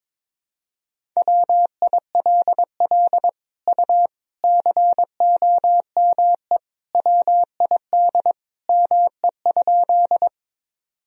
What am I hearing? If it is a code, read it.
Morse code, heard as a keyed tone, about 22 wpm, 700 Hz. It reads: WILL U COME WID ME?
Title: Sentences from Top 200 Words and Abbreviations 22wpm